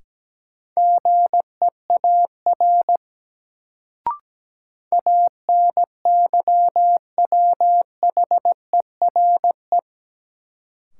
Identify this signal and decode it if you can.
Morse code, heard as a keyed tone, about 17 wpm, 700 Hz. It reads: GEAR ANYWHERE
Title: Sentences from Top 1400 Words - New Word Drill Alt - 17wpm